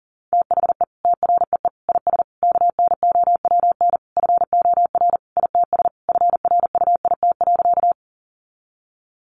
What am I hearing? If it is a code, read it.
Morse code, heard as a keyed tone, about 40 wpm, 700 Hz. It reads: THE TREE IS KNOWN FOR ITS FRUIT.
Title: Sentences from Top 1300 Words 40wpm